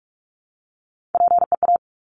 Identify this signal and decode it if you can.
Morse code, heard as a keyed tone, about 46 wpm, 700 Hz. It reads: PEA
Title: Top Three Letter Combinations in Words 45wpm